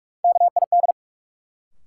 Morse code, heard as a keyed tone, about 46 wpm, 700 Hz. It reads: KID
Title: Sets of 5 Words - New Word Drill - 45wpm